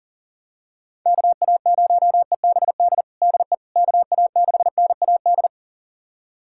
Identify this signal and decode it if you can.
Morse code, heard as a keyed tone, about 40 wpm, 700 Hz. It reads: KA0EBD DE KA6NAD
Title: USA Callsign DE Callsign - 6 Characters - 40wpm